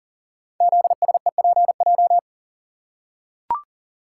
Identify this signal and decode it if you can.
Morse code, heard as a keyed tone, about 38 wpm, 690 Hz. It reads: ZSEPJ
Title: Code Groups - 5 Letters - 40wpm